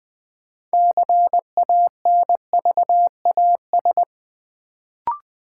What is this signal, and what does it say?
Morse code, heard as a keyed tone, about 20 wpm, 700 Hz. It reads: CANVAS
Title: Build A Word - 3 Letters At A Time - 6-Letter Words - 20wpm